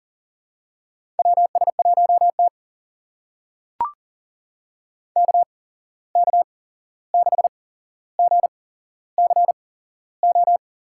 Morse code, heard as a keyed tone, about 40 wpm, spaced as for 15 wpm, 700 Hz. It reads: WS1T KK6GCO
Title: USA Call Signs - 8x Character Spacing - 40wpm